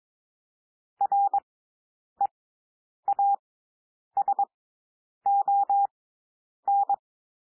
Morse code, heard as a keyed tone, about 22 wpm, spaced as for 9 wpm, 800 Hz. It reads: REASON
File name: SOUND WORDS 22 wpm